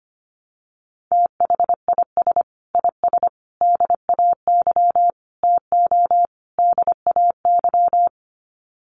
Morse code, heard as a keyed tone, about 25 wpm, 700 Hz. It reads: THIS IS DAY TO DAY
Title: Sets of 5 Words Encore 25wpm